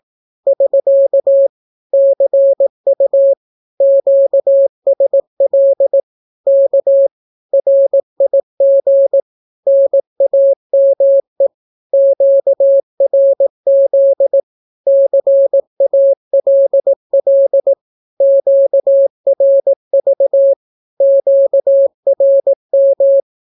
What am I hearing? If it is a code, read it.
Morse code, heard as a keyed tone, about 18 wpm, 550 Hz. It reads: <SK> CU QSL K RIG NAME QRZ CALL QRV QRM
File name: naukapodstaw / skroty / skroty